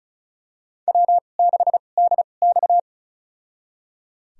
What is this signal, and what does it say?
Morse code, heard as a keyed tone, about 34 wpm, spaced as for 25 wpm, 700 Hz. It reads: W6DX
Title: USA Call Signs - 2x Character Spacing - 35wpm